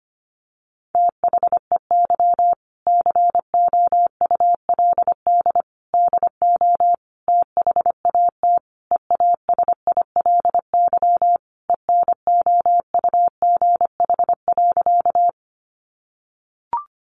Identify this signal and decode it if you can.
Morse code, heard as a keyed tone, about 25 wpm, 700 Hz. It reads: THEY COULD DO THAT EASILY ENOUGH.
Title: Sentences from Top 900 Words 25wpm